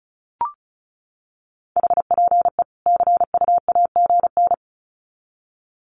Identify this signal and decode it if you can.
Morse code, heard as a keyed tone, about 34 wpm, 700 Hz. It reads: HPE CUAGN
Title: Common QSO Phrases 35wpm